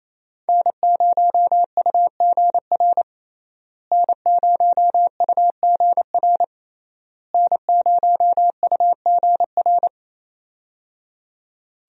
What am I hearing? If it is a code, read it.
Morse code, heard as a keyed tone, about 28 wpm, 700 Hz. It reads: N0UGR N0UGR N0UGR
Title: USA Call Signs - 5 Characters - Repeated 3x - 28wpm